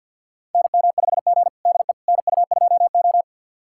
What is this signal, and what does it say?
Morse code, heard as a keyed tone, about 51 wpm, 700 Hz. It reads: NM5G DE NU2O
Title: USA Callsign DE Callsign - 4 Characters - 50wpm